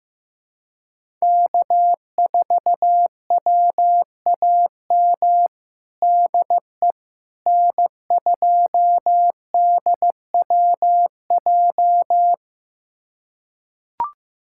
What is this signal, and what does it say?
Morse code, heard as a keyed tone, about 15 wpm, 700 Hz. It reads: K4WAM DE N2DWJ
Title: USA Callsign DE Callsign - 5 Characters - 15wpm